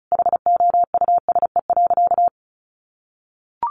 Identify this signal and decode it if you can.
Morse code, heard as a keyed tone, about 34 wpm, 700 Hz. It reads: HOUSE.
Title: Sentences from Top 1200 Words 35wpm